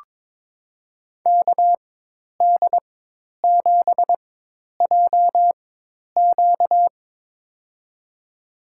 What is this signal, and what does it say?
Morse code, heard as a keyed tone, about 22 wpm, spaced as for 10 wpm, 700 Hz. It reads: KD7JQ
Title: USA Call Signs - 5 Characters - 4x Character Spacing - 22wpm